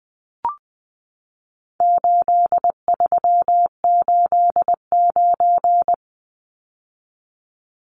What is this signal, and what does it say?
Morse code, heard as a keyed tone, about 20 wpm, 700 Hz. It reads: 8389
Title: Numbers - 4-Digit 20wpm